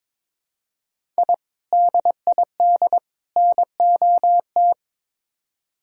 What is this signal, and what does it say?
Morse code, heard as a keyed tone, about 22 wpm, 700 Hz. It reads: I DID NOT
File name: practice / 22wpm / Build A Phrase - 5-Word Phrases - 22wpm